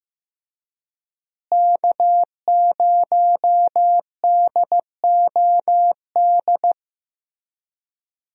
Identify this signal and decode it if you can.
Morse code, heard as a keyed tone, about 15 wpm, 700 Hz. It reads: K0DOD
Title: USA Call Signs - 5 Characters - Repeated 3x - 15wpm